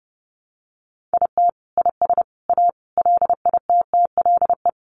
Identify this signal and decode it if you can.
Morse code, heard as a keyed tone, about 30 wpm, 700 Hz. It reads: IT IS A LITTLE
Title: Sentences from Top 1300 Words 30wpm